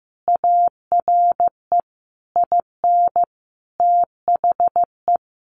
Morse code, heard as a keyed tone, about 15 wpm, 700 Hz. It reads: ARE IN THE
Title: Sentences from Top 1300 Words 15wpm